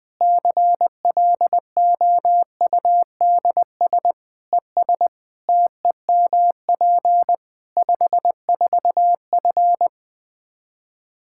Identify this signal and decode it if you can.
Morse code, heard as a keyed tone, about 20 wpm, 700 Hz. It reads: CLOUDS ES TEMP 54F
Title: Practice QSO - Exchange 2 - Info - 20wpm